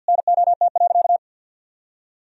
Morse code, heard as a keyed tone, about 50 wpm, 700 Hz. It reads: NOT.
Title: Sentences from Top 1000 Words - Review Part 3 - 50wpm